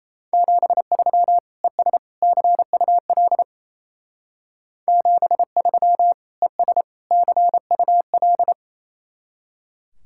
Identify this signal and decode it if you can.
Morse code, heard as a keyed tone, about 36 wpm, 700 Hz. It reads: 73 ES CUL 73 ES CUL